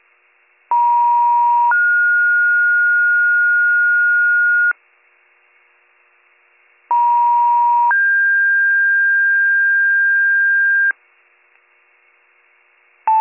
Two-tone page, heard as a one-second tone, then a three-second tone, about 950 then 1430 Hz, about 950 then 1600 Hz. It is heard from 0.7 to 10.9 s.